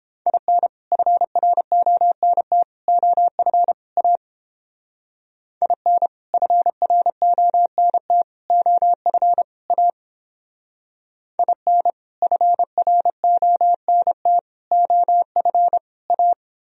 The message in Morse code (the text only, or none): IN FRONT OF A IN FRONT OF A IN FRONT OF A